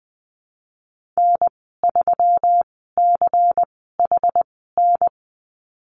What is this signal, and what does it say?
Morse code, heard as a keyed tone, about 20 wpm, spaced as for 15 wpm, 700 Hz. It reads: N3CHN
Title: 154 Lesson 25g - Letter p - Call Signs 20wpm